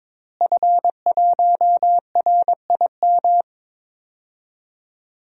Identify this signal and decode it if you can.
Morse code, heard as a keyed tone, about 22 wpm, 700 Hz. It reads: F1RIM
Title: DXCC-100 Call Signs - 5 Characters - Repeated 2x - 22wpm